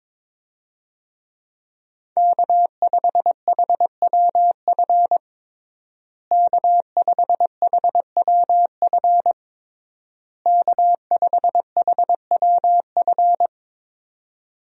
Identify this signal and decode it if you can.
Morse code, heard as a keyed tone, about 22 wpm, 700 Hz. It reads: K5HWF K5HWF K5HWF
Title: USA Call Signs - 5 Characters - Repeated 3x - 22wpm